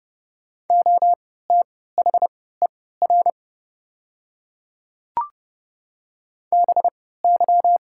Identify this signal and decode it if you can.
Morse code, heard as a keyed tone, about 30 wpm, spaced as for 17 wpm, 700 Hz. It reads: OTHER BY
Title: 3x Character Spacing - Top 100 Words - 30wpm